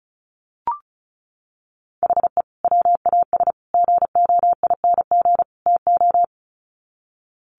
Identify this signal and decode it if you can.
Morse code, heard as a keyed tone, about 34 wpm, 700 Hz. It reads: HE WAS GOING TO
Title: Sets of 4 Words 35wpm